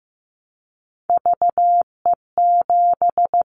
Morse code, heard as a keyed tone, about 15 wpm, 700 Hz. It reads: VE7
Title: Callsign Prefixes for Top 100 DXCC - 15wpm